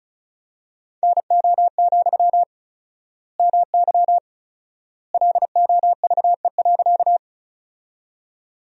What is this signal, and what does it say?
Morse code, heard as a keyed tone, about 36 wpm, 700 Hz. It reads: NO, MY LOVE.
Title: 4x Word-Spacing - Sentences from Top 400 Words - 35wpm